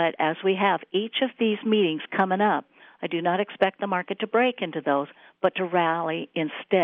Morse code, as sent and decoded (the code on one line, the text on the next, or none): none